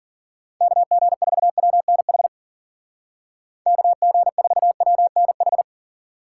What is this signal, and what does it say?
Morse code, heard as a keyed tone, about 52 wpm, 700 Hz. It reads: KG4WNH KG4WNH